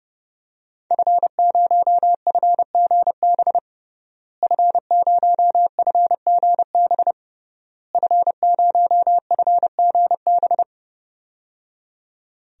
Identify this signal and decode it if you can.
Morse code, heard as a keyed tone, about 30 wpm, 700 Hz. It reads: F0FGB F0FGB F0FGB
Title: DXCC-100 Call Signs - 5 Characters - Repeated 3x - 30wpm